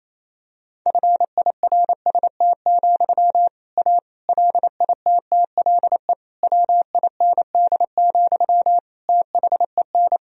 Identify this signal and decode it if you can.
Morse code, heard as a keyed tone, about 28 wpm, 700 Hz. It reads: FIRST, A LITTLE WIND, THEN
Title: Sentences from Top 1000 Words - Review Part 2 - 28wpm